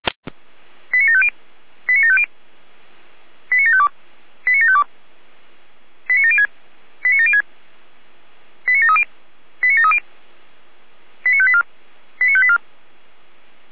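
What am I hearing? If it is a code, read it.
Five-tone selective call (ZVEI), heard as quick runs of five tones, tones of about 80 ms. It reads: 89750 89750 89752 89752 89706 89706 89730 89730 89584 89584
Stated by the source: ZVEI five-tone selective call received by SDR, audio archive of Feuerwehr Orlamünde (f85360).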